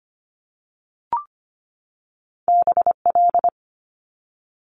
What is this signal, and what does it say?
Morse code, heard as a keyed tone, about 25 wpm, 700 Hz. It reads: BL
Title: Multi-letter Phonemes 25wpm